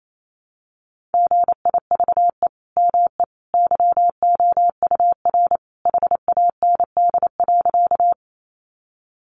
Morse code, heard as a keyed tone, about 28 wpm, 700 Hz. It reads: GIVE ME YOUR HAND.
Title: Sentences from Top 200 Words 28wpm